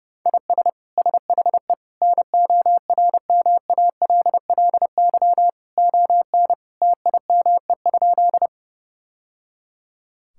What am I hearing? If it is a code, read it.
Morse code, heard as a keyed tone, about 32 wpm, 700 Hz. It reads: IS SHE NORMALLY ON TIME?
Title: Sentences from Top 1000 Words - Review Part 6 - 30wpm